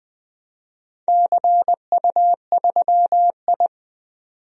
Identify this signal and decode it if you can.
Morse code, heard as a keyed tone, about 20 wpm, 700 Hz. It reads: CU3I
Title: DXCC-100 Call Signs - 4 Characters - 20wpm